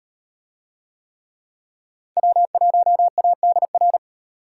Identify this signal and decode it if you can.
Morse code, heard as a keyed tone, about 37 wpm, 700 Hz. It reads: W1ADR